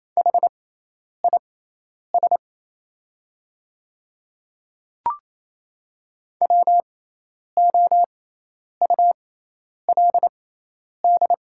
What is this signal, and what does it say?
Morse code, heard as a keyed tone, about 28 wpm, spaced as for 10 wpm, 700 Hz. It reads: HIS WOULD
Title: 6x Character Spacing - Top 300 Words - 28wpm